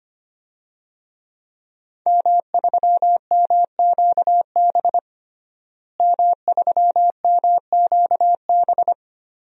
Morse code, heard as a keyed tone, about 25 wpm, 700 Hz. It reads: M3MQB M3MQB
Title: DXCC-100 Call Signs - 5 Characters - Repeated 3x - 25wpm